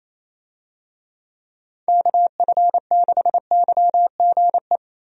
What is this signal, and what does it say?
Morse code, heard as a keyed tone, about 28 wpm, 700 Hz. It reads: KF6YGE
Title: USA Call Signs - 6 Characters - Repeated 2x - 28wpm